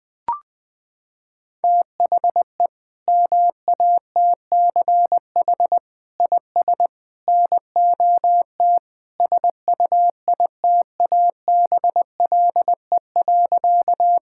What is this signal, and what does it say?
Morse code, heard as a keyed tone, about 20 wpm, 700 Hz. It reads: THE MATCH IS NOT SUITABLE.
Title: Sentences from Top 1600 Words 20wpm